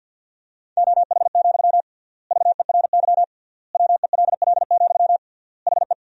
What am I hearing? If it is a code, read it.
Morse code, heard as a keyed tone, about 50 wpm, 700 Hz. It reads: OH, VERY WELL, HE
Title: 3x Word-Spacing - Sentences from Top 500 Words - 50wpm